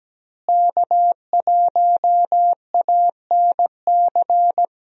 Morse code, heard as a keyed tone, about 17 wpm, 700 Hz. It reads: K1ANC
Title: USA Call Signs - 5 Characters - Repeated 2x - 17wpm